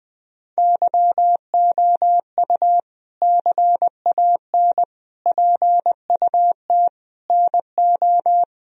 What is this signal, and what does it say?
Morse code, heard as a keyed tone, about 20 wpm, 700 Hz. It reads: YOU CAN PUT NO
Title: Sentences from Top 1000 Words - Review Part 2 - 20wpm